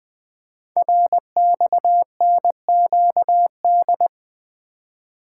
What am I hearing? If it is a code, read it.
Morse code, heard as a keyed tone, about 20 wpm, 700 Hz. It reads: RXNQD
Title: Code Groups - 5 Letters - 20wpm